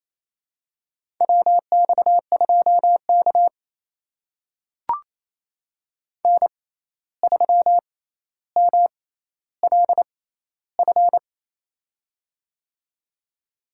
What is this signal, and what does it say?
Morse code, heard as a keyed tone, about 28 wpm, spaced as for 13 wpm, 700 Hz. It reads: WX2K N3MLF
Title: USA Call Signs - 6x Character Spacing - 28wpm